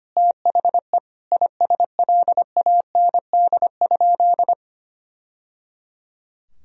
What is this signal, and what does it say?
Morse code, heard as a keyed tone, about 25 wpm, 700 Hz. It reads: THE ISLAND?
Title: Sentences from Top 1000 Words - Review Part 3 - 25wpm